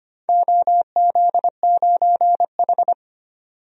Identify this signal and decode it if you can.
Morse code, heard as a keyed tone, about 25 wpm, 700 Hz. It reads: OZ9H
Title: DXCC-100 Call Signs - 4 Characters - 25wpm